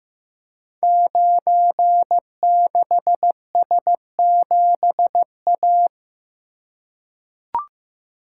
Morse code, heard as a keyed tone, about 15 wpm, 700 Hz. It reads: 96S7A
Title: Code Groups - 5 Letter:Numbers - 15wpm